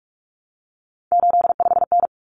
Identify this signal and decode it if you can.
Morse code, heard as a keyed tone, about 44 wpm, 700 Hz. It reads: 85N
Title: Common Cut Numbers - 3-Digit 45wpm